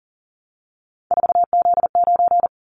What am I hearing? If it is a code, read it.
Morse code, heard as a keyed tone, about 40 wpm, 700 Hz. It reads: 4Z9
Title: Callsign Prefixes for Top 50 DXCC - 40wpm